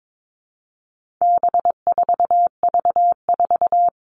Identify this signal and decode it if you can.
Morse code, heard as a keyed tone, about 22 wpm, 700 Hz. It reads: B4V4
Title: All Cut Numbers - 4-Digit 22wpm